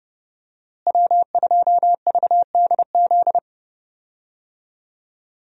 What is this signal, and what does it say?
Morse code, heard as a keyed tone, about 30 wpm, 700 Hz. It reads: W2VDZ
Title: USA Call Signs - 5 Characters - Repeated 2x - 30wpm